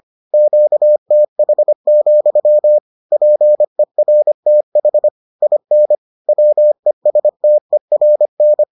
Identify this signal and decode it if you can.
Morse code, heard as a keyed tone, about 25 wpm, 600 Hz. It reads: QTH, PERTH IN WESTERN